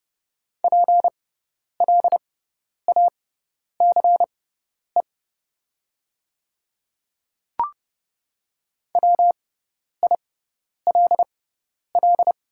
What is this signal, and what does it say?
Morse code, heard as a keyed tone, about 30 wpm, spaced as for 10 wpm, 700 Hz. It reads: PLACE WILL
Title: 6x Character Spacing - Top 200 Words - 30wpm